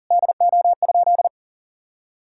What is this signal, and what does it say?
Morse code, heard as a keyed tone, about 40 wpm, 700 Hz. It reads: DO?
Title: Sentences from Top 1600 Words 40wpm